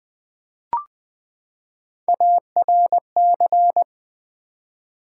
Morse code, heard as a keyed tone, about 20 wpm, 700 Hz. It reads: ARC
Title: Build A Word - 3 Letters At A Time - 9-Letter Words - 20wpm